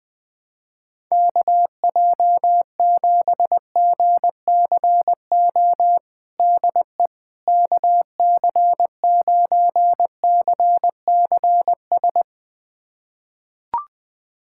Morse code, heard as a keyed tone, about 20 wpm, 700 Hz. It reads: KJ7GCO DE KC9CCS
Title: USA Callsign DE Callsign 20wpm